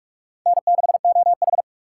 Morse code, heard as a keyed tone, about 44 wpm, 700 Hz. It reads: N6OH
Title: USA Call Signs - 4 Characters - 6x Character Spacing - 45wpm